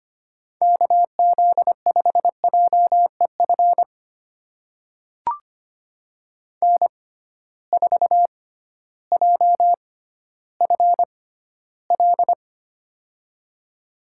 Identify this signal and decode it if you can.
Morse code, heard as a keyed tone, about 25 wpm, spaced as for 14 wpm, 700 Hz. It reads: KZ5JEF N4JFL
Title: USA Call Signs - 6x Character Spacing - 25wpm